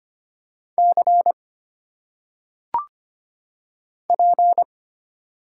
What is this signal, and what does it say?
Morse code, heard as a keyed tone, about 25 wpm, 700 Hz. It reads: C P
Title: Single Letters - Phonetic Pronunciation 25wpm